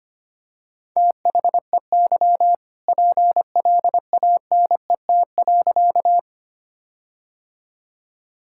Morse code, heard as a keyed tone, about 25 wpm, 700 Hz. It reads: THEY PLANET.
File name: practice / 25wpm / All-Age Appropriate Jokes - 25wpm